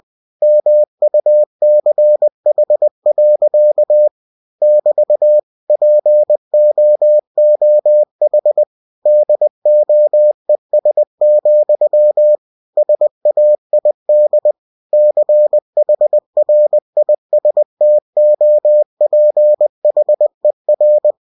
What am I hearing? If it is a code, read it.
Morse code, heard as a keyed tone, about 20 wpm, 600 Hz. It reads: MUCH. <BT> POOH DOES, SAID CHRISTOPHER
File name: cw-books / winnie-the-pooh / winnie-20wpm-page-0110